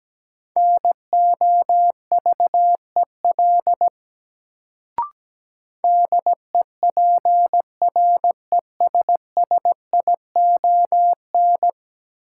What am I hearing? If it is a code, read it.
Morse code, heard as a keyed tone, about 17 wpm, 700 Hz. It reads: NOVEL DEPRESSION
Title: Sentences from Top 1300 Words - New Word Drill - 17wpm